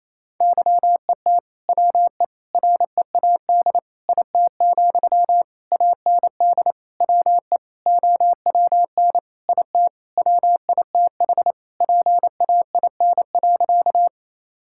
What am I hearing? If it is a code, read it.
Morse code, heard as a keyed tone, about 28 wpm, 700 Hz. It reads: YET WE READ IT, AND WE OWN IT WITH PAIN.
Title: Sentences from Top 1600 Words 28wpm